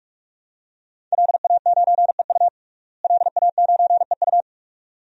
Morse code, heard as a keyed tone, about 43 wpm, 700 Hz. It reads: LA9EU LA9EU
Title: DXCC-100 Call Signs - 5 Characters - Repeated 2x - 45wpm